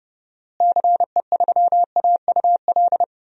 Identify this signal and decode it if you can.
Morse code, heard as a keyed tone, about 30 wpm, 700 Hz. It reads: CE3AUL
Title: DXCC-100 Call Signs - 6 Characters - Repeated 2x - 30wpm